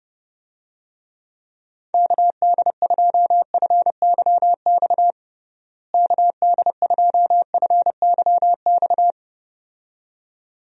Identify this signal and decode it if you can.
Morse code, heard as a keyed tone, about 30 wpm, 700 Hz. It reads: KD2FYX KD2FYX
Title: USA Call Signs - 6 Characters - Repeated 2x - 30wpm